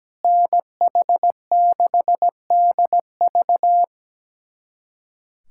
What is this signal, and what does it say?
Morse code, heard as a keyed tone, about 17 wpm, 700 Hz. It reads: NH6DV
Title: USA Callsign DE Callsign - 5 Characters - 17wpm